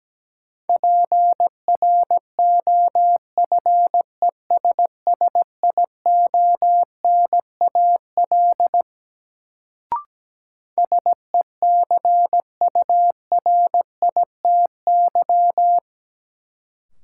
Morse code, heard as a keyed tone, about 17 wpm, 700 Hz. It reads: PROFESSIONAL SECURITY
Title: Sentences from Top 1000 Words - Review Part 4 - New Word Drill - 17wpm